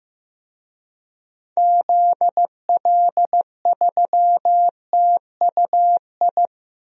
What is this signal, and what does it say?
Morse code, heard as a keyed tone, about 15 wpm, 700 Hz. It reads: ZL3TUI
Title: DXCC-100 Call Signs - 6 Characters - Repeated 2x - 15wpm